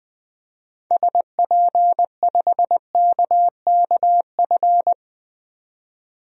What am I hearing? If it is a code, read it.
Morse code, heard as a keyed tone, about 20 wpm, 700 Hz. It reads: SP5KKF
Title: DXCC-100 Call Signs - 6 Characters - 20wpm